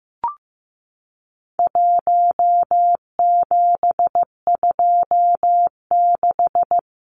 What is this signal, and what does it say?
Morse code, heard as a keyed tone, about 15 wpm, 700 Hz. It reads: 1726
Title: Numbers - 4-Digit 15wpm